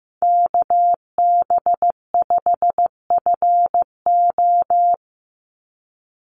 Morse code, heard as a keyed tone, about 15 wpm, 700 Hz. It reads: KB5FO
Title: USA Call Signs 15wpm